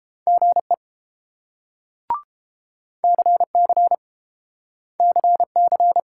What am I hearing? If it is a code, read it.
Morse code, heard as a keyed tone, about 30 wpm, 700 Hz. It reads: GE CC CC